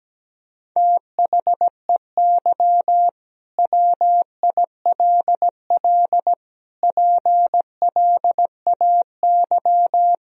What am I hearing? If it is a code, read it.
Morse code, heard as a keyed tone, about 17 wpm, 700 Hz. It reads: THEY WILL PLAY
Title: Sentences from Top 1000 Words - Review Part 4 - 17wpm